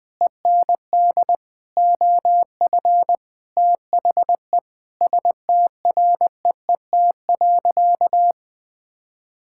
Morse code, heard as a keyed tone, about 21 wpm, 700 Hz. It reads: END OF THE STREET.
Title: Sentences with 5-Word Phrases - Spoken Beginning - 20wpm